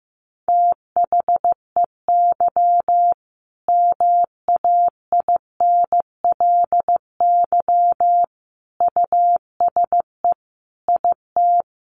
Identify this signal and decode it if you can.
Morse code, heard as a keyed tone, about 15 wpm, 700 Hz. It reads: THEY MAINLY USE IT
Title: Sentences from Top 1400 Words 15wpm